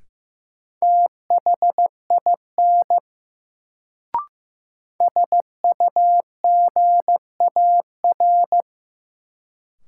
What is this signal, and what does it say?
Morse code, heard as a keyed tone, about 15 wpm, 700 Hz. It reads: THIN SUGAR
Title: Sentences from Top 1200 Words - New Word Drill Alt - 15wpm